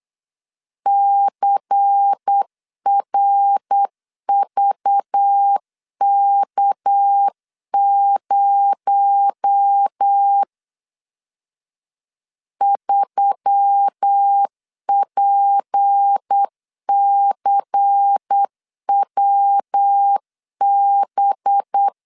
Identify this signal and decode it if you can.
Morse code, heard as a keyed tone, about 8 wpm, 800 Hz. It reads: CRVK0 3PCWB
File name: LEZIONI / 6 PKX0 / 6 PKX0 8wpm 3341